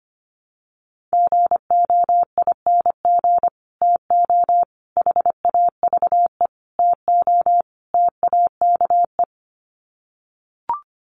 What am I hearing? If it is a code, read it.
Morse code, heard as a keyed tone, about 25 wpm, 700 Hz. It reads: GOING TO HAVE TO TAKE
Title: Sets of 5 Words Encore 25wpm